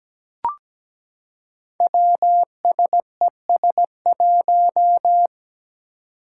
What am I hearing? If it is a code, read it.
Morse code, heard as a keyed tone, about 17 wpm, 700 Hz. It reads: WSES1
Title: Five Challenging Letter-Number Combinations 17wpm